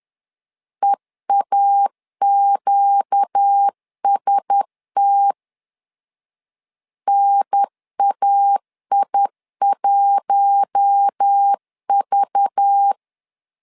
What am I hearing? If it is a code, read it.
Morse code, heard as a keyed tone, about 10 wpm, 800 Hz. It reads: EAQST NAI1V
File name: LEZIONI / 5 LBVC / 5 LBVC 10wpm 2349